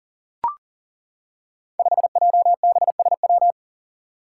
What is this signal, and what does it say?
Morse code, heard as a keyed tone, about 40 wpm, 700 Hz. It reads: 5JBSW
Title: Five Challenging Letter-Number Combinations 40wpm